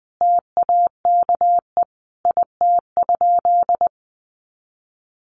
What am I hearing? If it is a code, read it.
Morse code, heard as a keyed tone, about 20 wpm, 700 Hz. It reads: TAKE IT?
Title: Sentences from Top 200 Words 20wpm